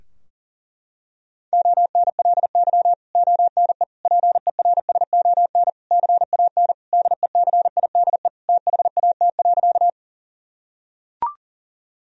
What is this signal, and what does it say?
Morse code, heard as a keyed tone, about 37 wpm, 700 Hz. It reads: ONLY ONE PERSON CAN DECIDE THAT.
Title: Sentences from Top 1000 Words - Review Part 3 - 40wpm